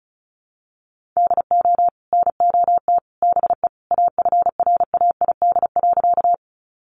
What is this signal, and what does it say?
Morse code, heard as a keyed tone, about 34 wpm, 700 Hz. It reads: DO NOT BE AFRAID.
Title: Sentences from Top 1500 Words 35wpm